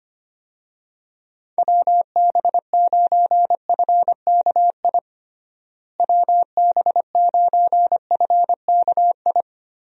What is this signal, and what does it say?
Morse code, heard as a keyed tone, about 25 wpm, 700 Hz. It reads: WB9FKI WB9FKI
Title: USA Call Signs - 6 Characters - Repeated 3x - 25wpm